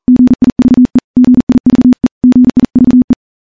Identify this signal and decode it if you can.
Morse code, heard as a keyed tone, about 40 wpm, 250 Hz. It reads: GIVE GIVE GIVE
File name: morse-training-40WPM